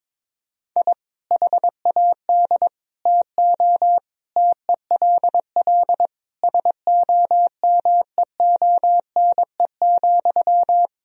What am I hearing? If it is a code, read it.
Morse code, heard as a keyed tone, about 22 wpm, 700 Hz. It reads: I HAD TO TELL SOMEONE,
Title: Sentences from Top 1000 Words - Review Part 4 - 22wpm